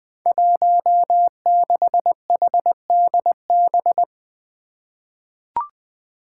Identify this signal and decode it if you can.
Morse code, heard as a keyed tone, about 20 wpm, 700 Hz. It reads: 16HDB
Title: Five Challenging Letter-Number Combinations 20wpm